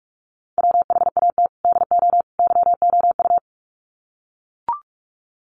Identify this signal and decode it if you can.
Morse code, heard as a keyed tone, about 44 wpm, 700 Hz. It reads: WHAT DO YOU
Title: Sets of 3 Words 45wpm